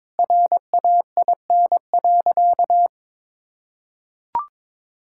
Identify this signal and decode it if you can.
Morse code, heard as a keyed tone, about 21 wpm, 700 Hz. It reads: RAIN.
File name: practice / 22wpm / Sentences with 2-Word Phrases - 22wpm